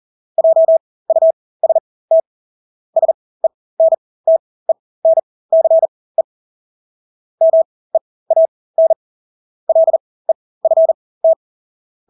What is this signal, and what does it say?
Morse code, heard as a keyed tone, about 40 wpm, spaced as for 20 wpm, 650 Hz. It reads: JUST SENTENCE MEAN LEFT
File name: Engelske ord 40-20 wpm